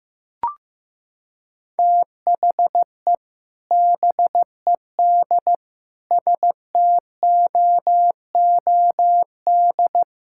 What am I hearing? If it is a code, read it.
Morse code, heard as a keyed tone, about 15 wpm, 700 Hz. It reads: THE BED STOOD
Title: Sentences from Top 1600 Words 15wpm